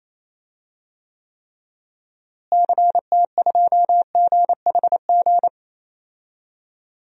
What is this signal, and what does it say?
Morse code, heard as a keyed tone, about 28 wpm, 700 Hz. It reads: CT2GHG
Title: DXCC-100 Call Signs - 6 Characters - Repeated 3x - 28wpm